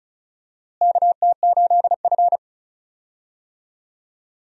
Morse code, heard as a keyed tone, about 36 wpm, 700 Hz. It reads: KT8F
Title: USA Call Signs - 4 Characters - Repeated 3x - 35wpm